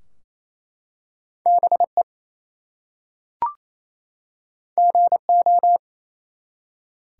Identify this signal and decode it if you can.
Morse code, heard as a keyed tone, about 27 wpm, 700 Hz. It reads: BE GO
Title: Two Letter Words - 28wpm